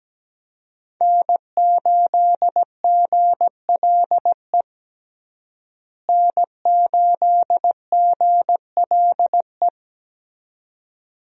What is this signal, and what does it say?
Morse code, heard as a keyed tone, about 17 wpm, 700 Hz. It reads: N8GLE N8GLE
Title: USA Call Signs - 5 Characters - Repeated 2x - 17wpm